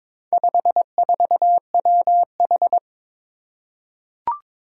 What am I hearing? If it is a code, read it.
Morse code, heard as a keyed tone, about 22 wpm, 700 Hz. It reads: 54WH
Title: Four Challenging Letter-Number Combinations 22wpm